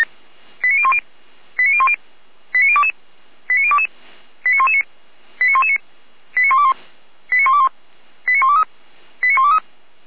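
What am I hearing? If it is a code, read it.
Five-tone selective call (ZVEI), heard as quick runs of five tones, tones of about 70 ms. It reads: 89010 89010 89022 89022 89119 89119 89121 89121 89123 89123
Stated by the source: ZVEI five-tone call 89011 received by SDR.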